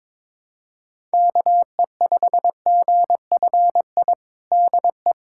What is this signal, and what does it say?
Morse code, heard as a keyed tone, about 22 wpm, 700 Hz. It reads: KE5GFI DE
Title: USA Callsign DE Callsign 22wpm